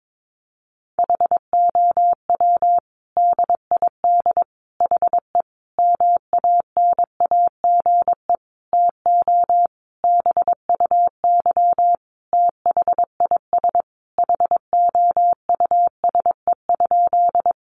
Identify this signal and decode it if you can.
Morse code, heard as a keyed tone, about 22 wpm, 700 Hz. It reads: HOW DID HE MANAGE TO BUY THIS HOUSE?
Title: Sentences from Top 1400 Words 22wpm